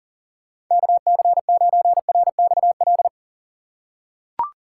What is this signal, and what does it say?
Morse code, heard as a keyed tone, about 40 wpm, 700 Hz. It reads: KC9RXL
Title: USA Call Signs - 6 Characters - 3x Character Spacing - 40wpm